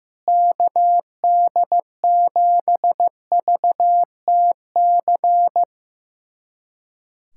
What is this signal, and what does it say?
Morse code, heard as a keyed tone, about 15 wpm, 700 Hz. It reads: KD7VTC
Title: USA Callsign DE Callsign - 6 Characters - 15wpm